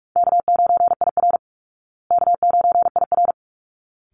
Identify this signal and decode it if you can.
Morse code, heard as a keyed tone, about 44 wpm, 700 Hz. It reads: K9IR K9IR
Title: Easy Call Signs Repeated 2x - 45wpm